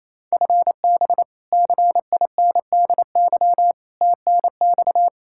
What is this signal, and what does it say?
Morse code, heard as a keyed tone, about 28 wpm, 700 Hz. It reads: FB CINDY TNX
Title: Practice QSO - Exchange 3 - More Info - 28wpm